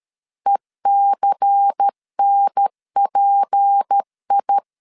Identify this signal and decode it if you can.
Morse code, heard as a keyed tone, about 13 wpm, 800 Hz. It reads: ECNPI
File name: LEZIONI / 6 PKX0 / 6 PKX0 12wpm 2231